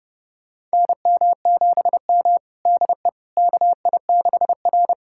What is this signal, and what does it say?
Morse code, heard as a keyed tone, about 30 wpm, 700 Hz. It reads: NM7M DE KI6R
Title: USA Callsign DE Callsign - 4 Characters - 30wpm